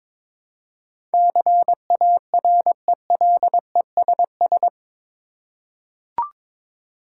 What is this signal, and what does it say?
Morse code, heard as a keyed tone, about 21 wpm, 700 Hz. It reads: CARELESS
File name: practice / 22wpm / Eight Letter Words - 22wpm